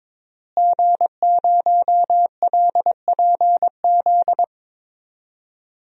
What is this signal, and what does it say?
Morse code, heard as a keyed tone, about 22 wpm, 700 Hz. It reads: G0LPZ
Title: DXCC-100 Call Signs - 5 Characters - Repeated 3x - 22wpm